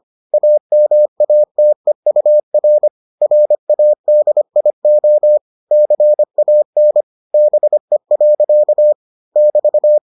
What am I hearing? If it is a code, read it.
Morse code, heard as a keyed tone, about 25 wpm, 600 Hz. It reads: AMATEUR RADIO CAN BE. <BT>